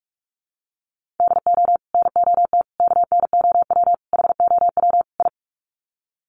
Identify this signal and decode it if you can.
Morse code, heard as a keyed tone, about 44 wpm, 700 Hz. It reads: DO NOT KNOW HOW I
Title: Sets of 5 Words Encore2 45wpm